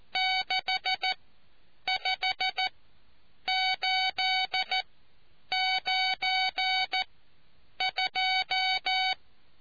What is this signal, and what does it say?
Morse code, heard as a keyed tone, about 14 wpm, spaced as for 8 wpm, 750 Hz. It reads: 65892